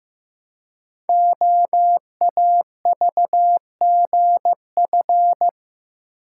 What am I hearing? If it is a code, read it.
Morse code, heard as a keyed tone, about 15 wpm, 700 Hz. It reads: OAVGF
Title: Code Groups - 5 Letters - 15wpm